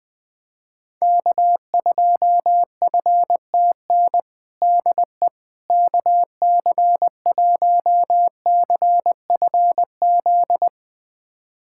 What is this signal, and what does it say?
Morse code, heard as a keyed tone, about 20 wpm, 700 Hz. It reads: K2FTN DE KC1CFZ
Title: USA Callsign DE Callsign 20wpm